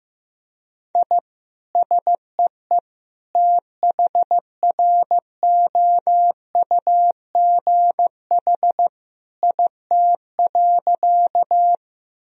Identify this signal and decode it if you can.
Morse code, heard as a keyed tone, about 15 wpm, 700 Hz. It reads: I SEE THROUGH IT.
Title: Sentences from Top 1000 Words - Review Part 1 - 15wpm